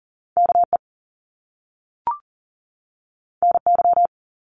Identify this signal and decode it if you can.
Morse code, heard as a keyed tone, about 42 wpm, 700 Hz. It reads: KE NY
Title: Top Two Letter Combinations in Words 40wpm